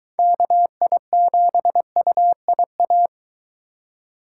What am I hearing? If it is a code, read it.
Morse code, heard as a keyed tone, about 23 wpm, 700 Hz. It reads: KI7UIA